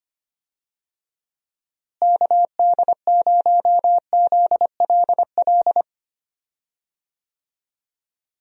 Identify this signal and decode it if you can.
Morse code, heard as a keyed tone, about 25 wpm, 700 Hz. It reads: KD0ZLL
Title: USA Call Signs - 6 Characters - Repeated 2x - 25wpm